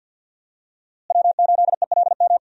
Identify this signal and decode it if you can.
Morse code, heard as a keyed tone, about 49 wpm, 700 Hz. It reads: W8ELM
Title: USA Call Signs - 5 Characters - Repeated 2x - 50wpm